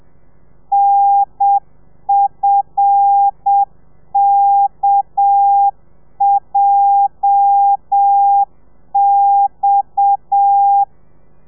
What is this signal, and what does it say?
Morse code, heard as a keyed tone, about 7 wpm, 800 Hz. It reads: NFKJX
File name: telegrafia / i2tfj / Lez8 35